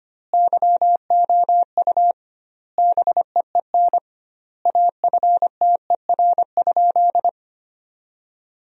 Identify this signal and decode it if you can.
Morse code, heard as a keyed tone, about 25 wpm, 700 Hz. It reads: YOU BEEN AFTER?
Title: 2x Word-Spacing - Sentences from Top 200 Words - 25wpm